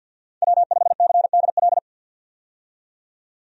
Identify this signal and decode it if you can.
Morse code, heard as a keyed tone, about 50 wpm, 700 Hz. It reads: W5CDL
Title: USA Call Signs - 5 Characters - 50wpm